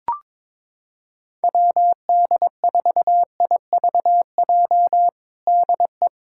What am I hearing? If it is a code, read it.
Morse code, heard as a keyed tone, about 22 wpm, 700 Hz. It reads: WD4IVJ DE
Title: USA Callsign DE Callsign 22wpm